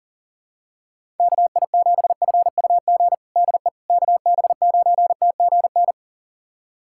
Morse code, heard as a keyed tone, about 40 wpm, 700 Hz. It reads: KI7FUG DE KB9TGN
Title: USA Callsign DE Callsign - 6 Characters - 40wpm